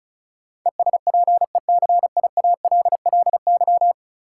Morse code, heard as a keyed tone, about 36 wpm, 700 Hz. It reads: ESPECIALLY
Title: Sentences from Top 700 Words - New Word Drill - 35wpm